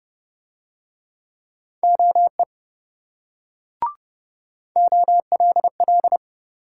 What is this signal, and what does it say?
Morse code, heard as a keyed tone, about 26 wpm, 700 Hz. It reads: OE OLL